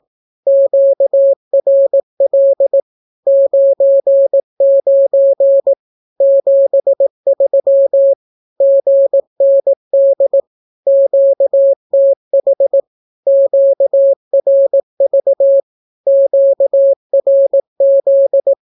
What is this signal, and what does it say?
Morse code, heard as a keyed tone, about 18 wpm, 550 Hz. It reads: QRL 99 73 GND QTH QRV QRZ
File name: naukapodstaw / skroty / skroty